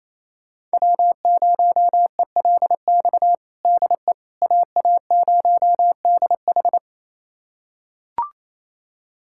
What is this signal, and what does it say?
Morse code, heard as a keyed tone, about 28 wpm, 700 Hz. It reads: W0ELX DE AA0DH
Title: USA Callsign DE Callsign - 5 Characters - 28wpm